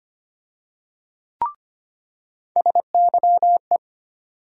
Morse code, heard as a keyed tone, about 25 wpm, 700 Hz. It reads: SYE
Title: USA Call Sign Postfixes - 3 Letters 25wpm